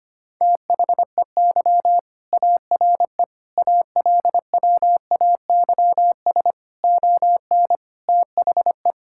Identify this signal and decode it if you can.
Morse code, heard as a keyed tone, about 25 wpm, 700 Hz. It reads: THEY ARE ALWAYS ON THE
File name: practice / 25wpm / All-Age Appropriate Jokes - 25wpm